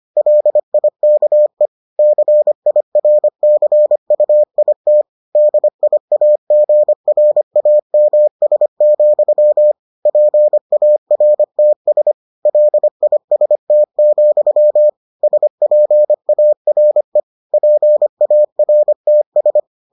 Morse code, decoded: LIKE CIRCUIT DIAGRAMS, PARTS LIST, SPARE PARTS

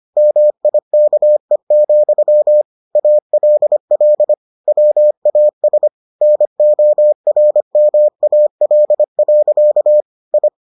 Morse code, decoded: MIKE, ALL WAS NORMAL. I

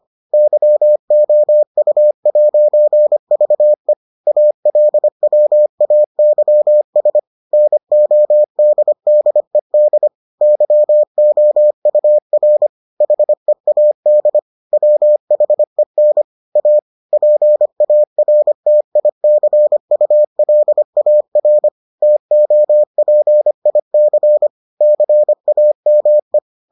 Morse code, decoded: YOU'VE ALWAYS NODDED YOUR HEAD WHEN A PARTICULAR TOPIC CAME

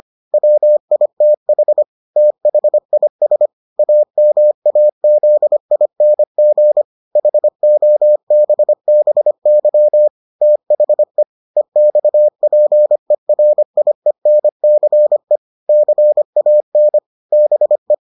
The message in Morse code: WITH THIS AMAZING HOBBY THE EXPERIENCE CAN BE

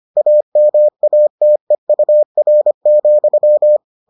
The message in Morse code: AMATEUR,

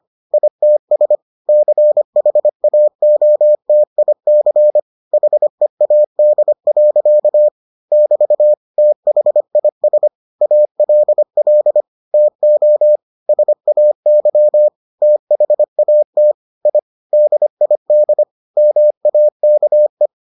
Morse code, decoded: ITS CHAOTIC HEAD. <BT> THIS ALL TO SAY THAT I DID MAKE

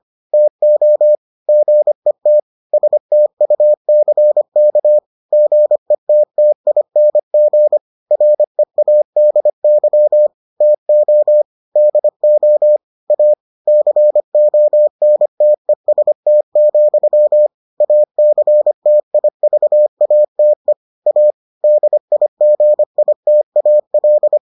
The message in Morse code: TO GET STUCK GETTING READY TO DO A CONTEST, ACTIVATE A DIGITAL

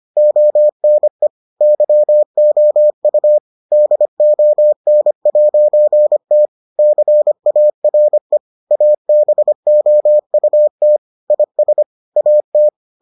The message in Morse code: ONE YOU DON'T CARE ABOUT IS AT

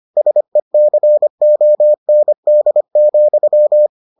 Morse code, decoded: SECOND,

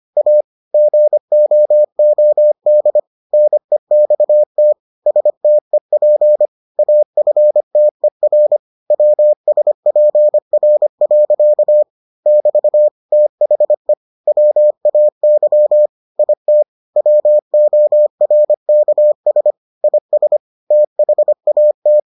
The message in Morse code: A GOOD NEXT STEP AFTER WSPR. <BT> THE WAY IT WORKS IS THAT

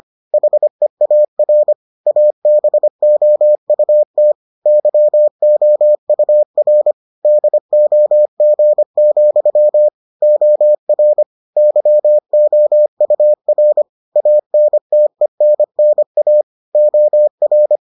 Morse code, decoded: HEAR ABOUT YOUR DOG, OR YOUR ANTENNA OR

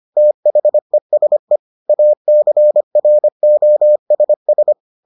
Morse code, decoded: THESE ACROSS

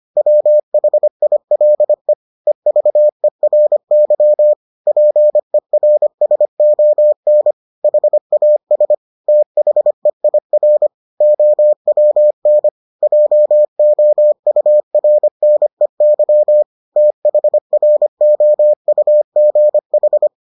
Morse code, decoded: WHILE EVERY PERSON HAS THEIR OWN JOURNEY THROUGH

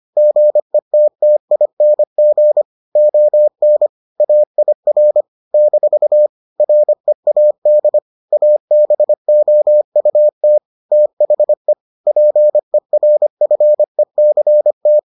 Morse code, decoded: GETTING ON AIR - READ ABOUT THE PERFECT

